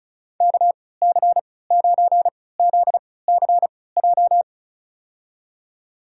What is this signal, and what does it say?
Morse code, heard as a keyed tone, about 36 wpm, spaced as for 20 wpm, 700 Hz. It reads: KC9ZCJ
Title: USA Call Signs - 6 Characters - 3x Character Spacing - 35wpm